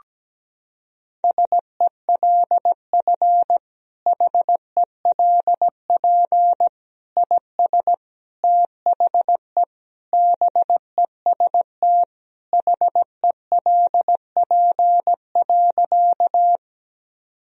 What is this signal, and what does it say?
Morse code, heard as a keyed tone, about 17 wpm, 700 Hz. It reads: SELF HELP IS THE BEST HELP.
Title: Sentences from Top 1000 Words - Review Part 1 - 17wpm